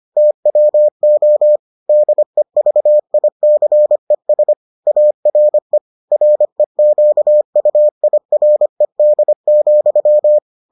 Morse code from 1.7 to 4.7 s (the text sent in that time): DEVICES